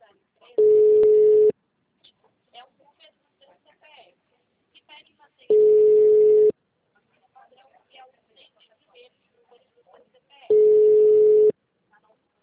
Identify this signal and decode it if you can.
Ringback tone (Europe, 425 Hz), with speech.